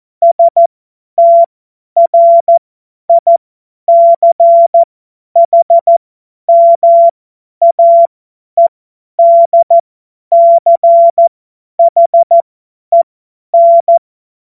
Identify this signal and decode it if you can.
Morse code, heard as a keyed tone, about 14 wpm, spaced as for 10 wpm, 680 Hz. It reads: STRICHMAEDCHEN
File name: Morsekurs / Witze / Witz 19. April 2022 50 F70